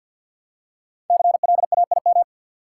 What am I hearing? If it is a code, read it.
Morse code, heard as a keyed tone, about 53 wpm, 700 Hz. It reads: CLAIM
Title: Five Letter Words - 50wpm